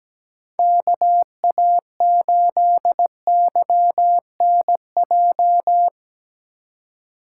Morse code, heard as a keyed tone, about 17 wpm, 700 Hz. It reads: KA8YNJ
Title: USA Call Signs - 3x Character Spacing - 17wpm